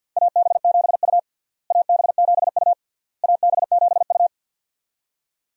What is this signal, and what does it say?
Morse code, heard as a keyed tone, about 53 wpm, 700 Hz. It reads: AB7U AB7U AB7U
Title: USA Call Signs - 4 Characters - Repeated 3x - 50wpm